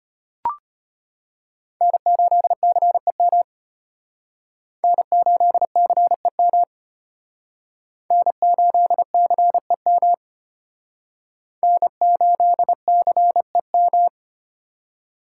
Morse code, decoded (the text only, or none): N8CEM N8CEM N8CEM N8CEM